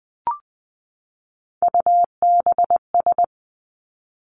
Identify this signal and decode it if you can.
Morse code, heard as a keyed tone, about 20 wpm, 700 Hz. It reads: UBS
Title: Three Challenging Letter-Number Combinations 20wpm